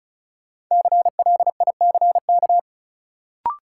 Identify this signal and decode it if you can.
Morse code, heard as a keyed tone, about 36 wpm, 700 Hz. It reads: CLICK
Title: Sentences from Top 1100 Words - New Word Drill Alt - 35wpm